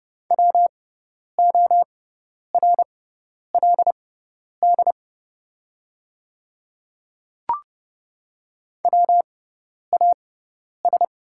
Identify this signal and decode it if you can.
Morse code, heard as a keyed tone, about 29 wpm, spaced as for 10 wpm, 690 Hz. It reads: WORLD WAS
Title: 6x Character Spacing - Top 200 Words - 30wpm